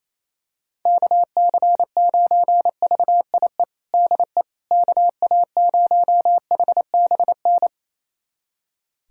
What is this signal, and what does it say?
Morse code, heard as a keyed tone, about 27 wpm, 700 Hz. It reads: KC9VIE DE KA0HBN
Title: USA Callsign DE Callsign - 6 Characters - 28wpm